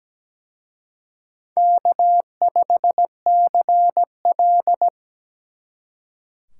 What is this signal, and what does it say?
Morse code, heard as a keyed tone, about 20 wpm, 700 Hz. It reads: K5CL
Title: Speed Racing - USA 1x2 Call Signs 17wpm